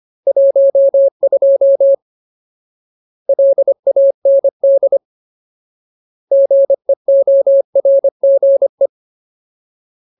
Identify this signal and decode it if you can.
Morse code, heard as a keyed tone, about 25 wpm, 550 Hz. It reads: 12 LAND GEORGE
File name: Ultimate Word List / 25WPM Ultimate